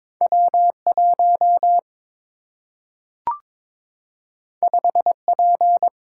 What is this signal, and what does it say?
Morse code, heard as a keyed tone, about 22 wpm, 700 Hz. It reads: W1 5P
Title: Single-Letter Single-Number 22wpm